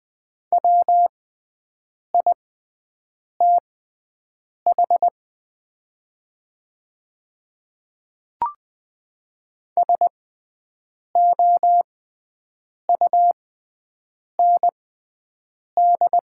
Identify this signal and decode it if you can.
Morse code, heard as a keyed tone, about 20 wpm, spaced as for 7 wpm, 700 Hz. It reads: WITH SOUND
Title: 6x Character Spacing - Top 100 Words - 20wpm